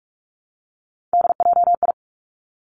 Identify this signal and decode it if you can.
Morse code, heard as a keyed tone, about 46 wpm, 700 Hz. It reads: DJI
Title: Three Challenging Letter-Number Combinations 45wpm